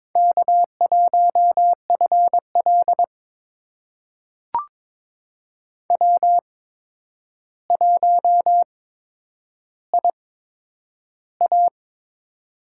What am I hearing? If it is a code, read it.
Morse code, heard as a keyed tone, about 22 wpm, spaced as for 10 wpm, 700 Hz. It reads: K1FL W1IA
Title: USA Call Signs - 4 Characters - 8x Character Spacing - 22wpm